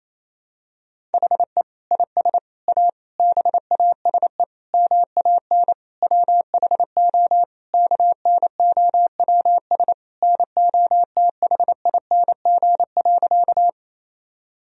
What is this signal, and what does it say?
Morse code, heard as a keyed tone, about 27 wpm, 700 Hz. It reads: HE IS A BASE MAN WHO KNOWS NOTHING.
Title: Sentences from Top 1000 Words - Review Part 1 - 28wpm